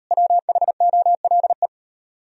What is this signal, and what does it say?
Morse code, heard as a keyed tone, about 36 wpm, 700 Hz. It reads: WHOLE